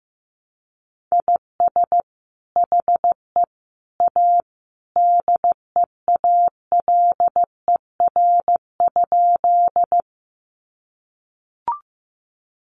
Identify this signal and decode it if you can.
Morse code, heard as a keyed tone, about 15 wpm, 700 Hz. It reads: IS HE A DEALER?
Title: Sentences from Top 1100 Words 15wpm